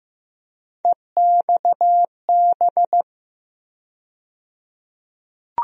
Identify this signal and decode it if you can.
Morse code, heard as a keyed tone, about 15 wpm, 700 Hz. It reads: EXB
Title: USA Call Sign Postfixes - 3 Letters 15wpm